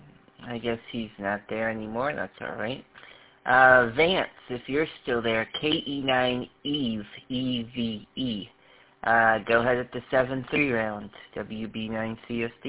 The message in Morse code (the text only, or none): none